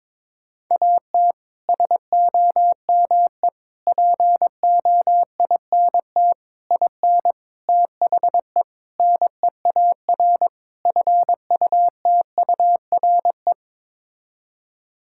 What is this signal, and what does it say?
Morse code, heard as a keyed tone, about 22 wpm, 700 Hz. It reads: AT SOME POINT IN THE NEAR FUTURE
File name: practice / 22wpm / Sets of 7 Words - 22wpm